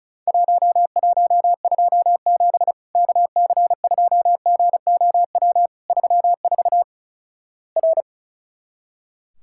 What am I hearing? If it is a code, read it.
Morse code, heard as a keyed tone, about 36 wpm, 700 Hz. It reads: 1127 KC2GOW 34 R
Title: Worked All Europe DX Contest - QTC Exchange - 35wpm